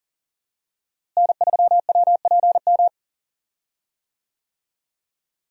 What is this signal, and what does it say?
Morse code, heard as a keyed tone, about 40 wpm, 700 Hz. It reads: N3WPM
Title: USA Call Signs - 5 Characters - Repeated 3x - 40wpm